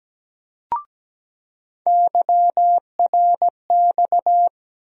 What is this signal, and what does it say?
Morse code, heard as a keyed tone, about 17 wpm, 700 Hz. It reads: YRX
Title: USA Call Sign Postfixes - 3 Letters 17wpm